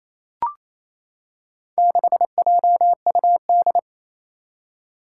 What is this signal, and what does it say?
Morse code, heard as a keyed tone, about 28 wpm, 700 Hz. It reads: 6JUD
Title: Four Challenging Letter-Number Combinations 28wpm